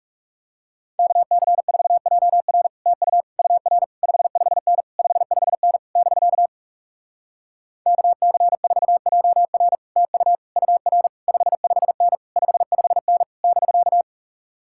Morse code, heard as a keyed tone, about 40 wpm, 700 Hz. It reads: KC4JR TU UR 55N 55N <BK> KC4JR TU UR 55N 55N <BK>